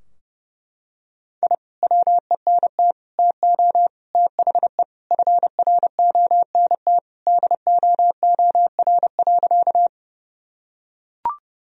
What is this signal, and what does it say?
Morse code, heard as a keyed tone, about 30 wpm, 700 Hz. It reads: I WENT TO THE FRONT DOOR.
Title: Sentences from Top 1000 Words - Review Part 2 - 30wpm